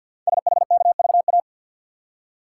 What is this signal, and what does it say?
Morse code, heard as a keyed tone, about 49 wpm, 700 Hz. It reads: SHKVA
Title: Code Groups - 5 Letters - 50wpm